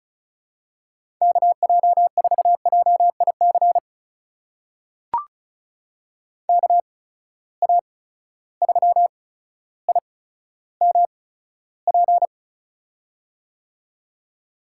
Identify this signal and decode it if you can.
Morse code, heard as a keyed tone, about 36 wpm, spaced as for 15 wpm, 700 Hz. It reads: KJ4JIC KA3IMP
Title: USA Call Signs - 6 Characters - 8x Character Spacing - 35wpm